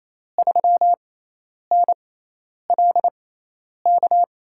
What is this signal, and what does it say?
Morse code, heard as a keyed tone, about 28 wpm, 700 Hz. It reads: 3 N L K
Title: USA Call Signs - 6x Character Spacing - 28wpm